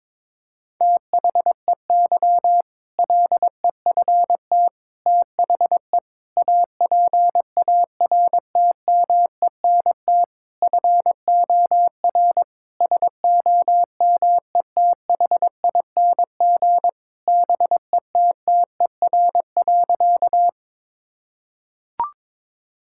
Morse code, heard as a keyed tone, about 22 wpm, 700 Hz. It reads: THEY LEFT THE APARTMENT FOR SOMETHING BETTER.
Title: Sentences from Top 1000 Words - Review Part 6 - 22wpm